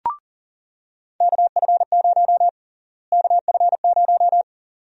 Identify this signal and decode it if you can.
Morse code, heard as a keyed tone, about 37 wpm, 700 Hz. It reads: KF0 KF0
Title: USA Call Sign Prefixes - Repeated 2x - 40wpm